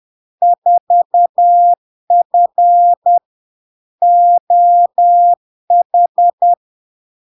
Morse code, heard as a keyed tone, about 10 wpm, 700 Hz. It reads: 4F OH
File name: LARK-FD-2022-Log-in-CW-10wpm-700-Hz